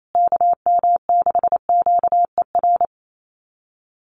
Morse code, heard as a keyed tone, about 28 wpm, 690 Hz. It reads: KM6QER
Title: USA Call Signs Repeated 3x 28wpm